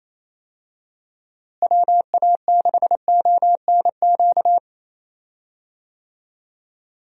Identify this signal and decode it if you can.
Morse code, heard as a keyed tone, about 27 wpm, 700 Hz. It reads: WA6ONQ
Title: USA Call Signs - 6 Characters - Repeated 2x - 28wpm